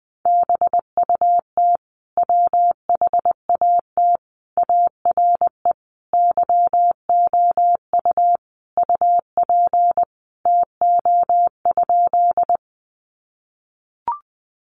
Morse code, heard as a keyed tone, about 20 wpm, 700 Hz. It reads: BUT WHAT ARE YOU UP TO?
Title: Sentences from Top 100 Words 20wpm